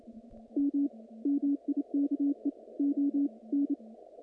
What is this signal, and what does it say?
Morse code, heard as a keyed tone, about 27 wpm, 280 Hz. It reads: M MIKE O N
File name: contests / naqp-cw-k5tr-op-k5pi- -2300